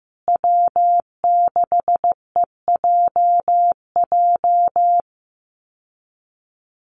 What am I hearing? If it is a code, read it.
Morse code, heard as a keyed tone, about 15 wpm, 700 Hz. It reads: W6EJJ
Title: USA Call Signs Repeated 3x 15wpm